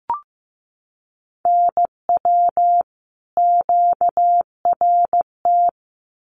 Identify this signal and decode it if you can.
Morse code, heard as a keyed tone, about 15 wpm, 700 Hz. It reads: NW QRT
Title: Common QSO Phrases 15wpm